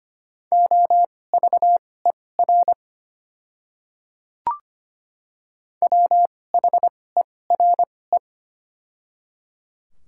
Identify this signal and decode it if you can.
Morse code, heard as a keyed tone, about 25 wpm, spaced as for 18 wpm, 700 Hz. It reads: OVER WHERE
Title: 2x Character Spacing - Top 200 Words - 25wpm